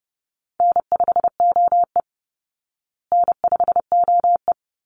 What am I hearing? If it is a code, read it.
Morse code, heard as a keyed tone, about 30 wpm, 700 Hz. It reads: N5OE N5OE
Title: Easy Call Signs Repeated 2x - 30wpm